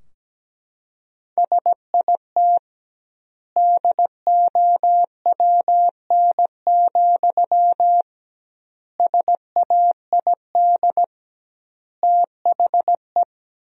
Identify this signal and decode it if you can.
Morse code, heard as a keyed tone, about 17 wpm, 700 Hz. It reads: SIT DOWN, SAID THE
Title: 2x Word-Spacing - Sentences from Top 500 Words - 17wpm